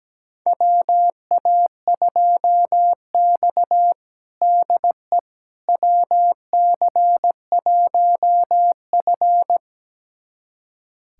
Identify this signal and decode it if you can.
Morse code, heard as a keyed tone, about 17 wpm, 700 Hz. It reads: WA2X DE WC1F
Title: USA Callsign DE Callsign - 4 Characters - 17wpm